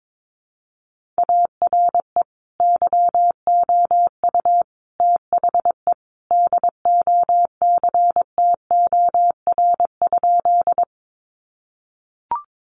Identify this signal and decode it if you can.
Morse code, heard as a keyed tone, about 22 wpm, 700 Hz. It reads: ARE YOU THE DOCTOR?
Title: Sentences from Top 1500 Words 22wpm